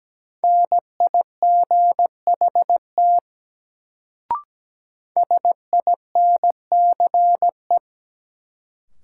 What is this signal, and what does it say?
Morse code, heard as a keyed tone, about 17 wpm, 700 Hz. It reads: NIGHT SINCE
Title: Sentences from Top 300 Words - New Word Drill - 17wpm